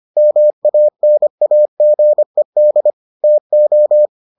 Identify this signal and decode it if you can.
Morse code, heard as a keyed tone, about 25 wpm, 600 Hz. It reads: MANAGED TO